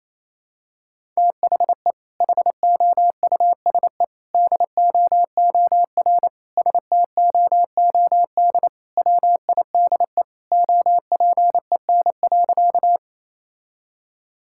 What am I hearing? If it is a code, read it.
Morse code, heard as a keyed tone, about 28 wpm, 700 Hz. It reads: THE HOUSE DOOR STOOD WIDE OPEN.
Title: Sentences from Top 1000 Words - Review Part 6 - 28wpm